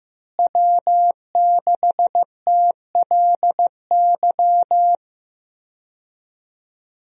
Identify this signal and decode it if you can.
Morse code, heard as a keyed tone, about 15 wpm, 700 Hz. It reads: W6TLY
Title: USA Call Signs - 5 Characters - Repeated 3x - 15wpm